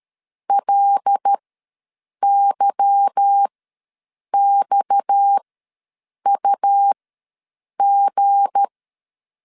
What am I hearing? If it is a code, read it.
Morse code, heard as a keyed tone, about 13 wpm, spaced as for 7 wpm, 800 Hz. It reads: LYXUG
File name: LEZIONI / 8 YJ45 / 8 Y ws 12wpm 3409